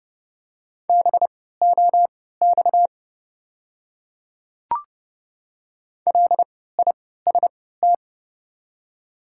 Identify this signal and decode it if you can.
Morse code, heard as a keyed tone, about 30 wpm, spaced as for 17 wpm, 690 Hz. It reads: BOX LIST
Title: 3x Character Spacing - Top 500 Words - 30wpm